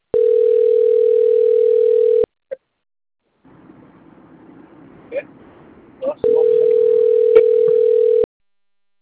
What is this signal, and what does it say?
Ringback tone (North America), 440 + 480 Hz.